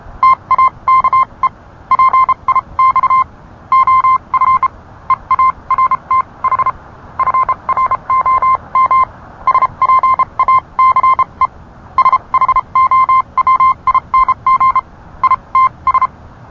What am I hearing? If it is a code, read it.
Morse code, heard as a keyed tone, about 31 wpm, spaced as for 26 wpm, 1050 Hz. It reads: TAKE PIX OF EARTH FROM SPACE SHOWING ITS